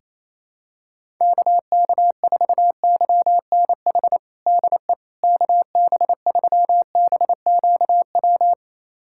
Morse code, heard as a keyed tone, about 28 wpm, 700 Hz. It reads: KK4YNH DE KB3BQW
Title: USA Callsign DE Callsign - 6 Characters - 28wpm